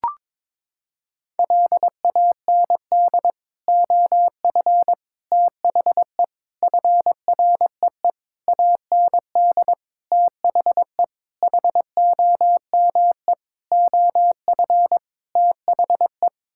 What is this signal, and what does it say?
Morse code, heard as a keyed tone, about 22 wpm, 700 Hz. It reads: LAND OF THE FREE AND THE HOME OF THE
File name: Sets of 9 Words - 22wpm